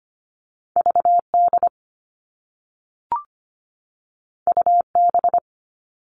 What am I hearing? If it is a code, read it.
Morse code, heard as a keyed tone, about 25 wpm, 700 Hz. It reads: VD UB
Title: All Cut Numbers - 2-Digit 25wpm